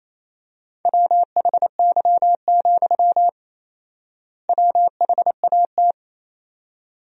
Morse code, heard as a keyed tone, about 28 wpm, 700 Hz. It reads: WHY, WHAT
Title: 4x Word-Spacing - Sentences from Top 200 Words - 28wpm